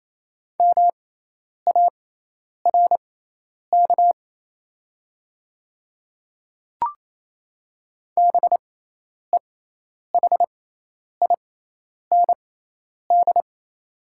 Morse code, heard as a keyed tone, about 28 wpm, spaced as for 10 wpm, 700 Hz. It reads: MARK BEHIND
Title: 6x Character Spacing - Top 500 Words - 28wpm